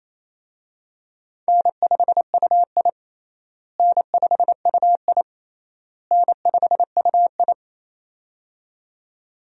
Morse code, heard as a keyed tone, about 27 wpm, 700 Hz. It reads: N5UI N5UI N5UI
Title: USA Call Signs - 4 Characters - Repeated 3x - 28wpm